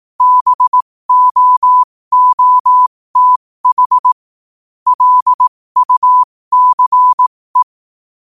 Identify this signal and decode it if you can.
Morse code, heard as a keyed tone, about 18 wpm, spaced as for 15 wpm, 1000 Hz. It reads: BOOTH LUCE
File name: qotdfiles / 15wpm